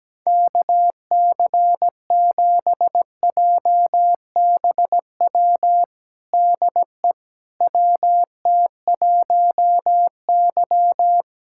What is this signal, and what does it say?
Morse code, heard as a keyed tone, about 17 wpm, 700 Hz. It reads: KC7JBW DE WT1Y
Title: USA Callsign DE Callsign 17wpm